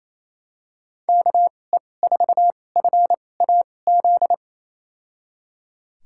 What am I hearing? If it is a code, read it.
Morse code, heard as a keyed tone, about 28 wpm, spaced as for 20 wpm, 700 Hz. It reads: KE4FAZ
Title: USA Call Signs - 2x Character Spacing - 28wpm